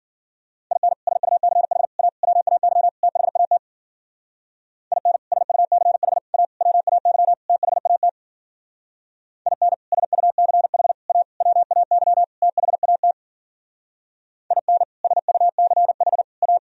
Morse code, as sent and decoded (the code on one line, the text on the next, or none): .. -. / ... ..- -.-. .... / .- / .-- .- -.-- / - .... .- - / .. -. / ... ..- -.-. .... / .- / .-- .- -.-- / - .... .- - / .. -. / ... ..- -.-. .... / .- / .-- .- -.-- / - .... .- - / .. -. / ... ..- -.-. .... / .-
IN SUCH A WAY THAT IN SUCH A WAY THAT IN SUCH A WAY THAT IN SUCH A